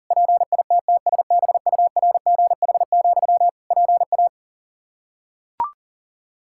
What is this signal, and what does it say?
Morse code, heard as a keyed tone, about 40 wpm, 700 Hz. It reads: PITTSBURGH, PA
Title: Practice QSO - Exchange 1 - Beginning - 40wpm